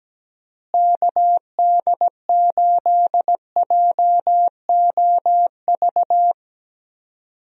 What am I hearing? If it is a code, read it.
Morse code, heard as a keyed tone, about 17 wpm, 700 Hz. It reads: KD8JOV
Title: USA Call Signs - 6 Characters - Repeated 3x - 17wpm